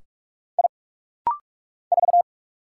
Morse code, heard as a keyed tone, about 46 wpm, 700 Hz. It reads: I 4
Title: Single Letter-Number - Rapid-Fire 45wpm